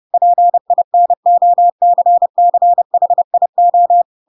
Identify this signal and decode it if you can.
Morse code, heard as a keyed tone, about 30 wpm, 700 Hz. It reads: PINOCCHIO
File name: Pinocchio / Pinocchio ch7 30 30 ik1hge